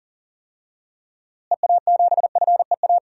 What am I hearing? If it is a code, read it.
Morse code, heard as a keyed tone, about 39 wpm, 700 Hz. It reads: EA7FEA
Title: DXCC-100 Call Signs - 6 Characters - Repeated 3x - 40wpm